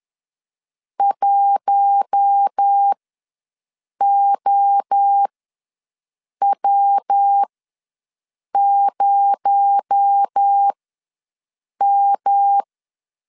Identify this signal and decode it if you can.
Morse code, heard as a keyed tone, about 10 wpm, spaced as for 6 wpm, 800 Hz. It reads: 1OW0M
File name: LEZIONI / 6 PKX0 / 6 PKX0 ws 10wpm 4220